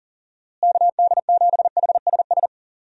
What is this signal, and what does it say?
Morse code, heard as a keyed tone, about 40 wpm, 700 Hz. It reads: KD7HSS
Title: USA Call Signs - 6 Characters - Repeated 2x - 40wpm